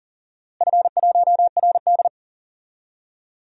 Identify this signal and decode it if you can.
Morse code, heard as a keyed tone, about 40 wpm, 700 Hz. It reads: F1RD
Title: DXCC-100 Call Signs - 4 Characters - Repeated 2x - 40wpm